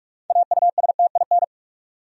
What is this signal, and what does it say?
Morse code, heard as a keyed tone, about 46 wpm, 700 Hz. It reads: AUSTIN
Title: USA Common Names - 45wpm